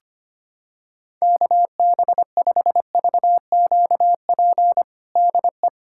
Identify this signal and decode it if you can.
Morse code, heard as a keyed tone, about 25 wpm, 700 Hz. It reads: KB5VQP DE
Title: USA Callsign DE Callsign - 6 Characters - 25wpm